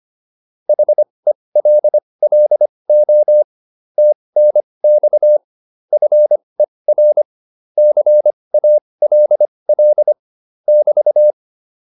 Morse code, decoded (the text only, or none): HELLO TNX FER CALL <BT>